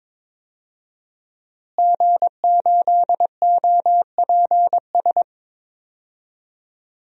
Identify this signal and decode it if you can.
Morse code, heard as a keyed tone, about 21 wpm, 700 Hz. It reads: G8OPS
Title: DXCC-100 Call Signs - 5 Characters - Repeated 2x - 22wpm